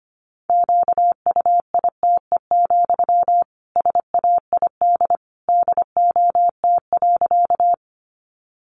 Morse code, heard as a keyed tone, about 25 wpm, 700 Hz. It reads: QUITE, SAID DOT.
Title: Sentences from Top 1500 Words 25wpm